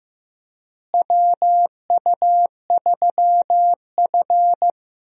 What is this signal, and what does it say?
Morse code, heard as a keyed tone, about 16 wpm, 700 Hz. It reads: WU3F
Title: USA Call Signs - 4 Characters - Repeated 3x - 15wpm